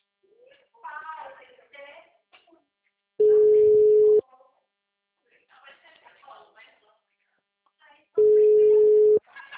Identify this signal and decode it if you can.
Ringback tone (Europe, 425 Hz), with speech.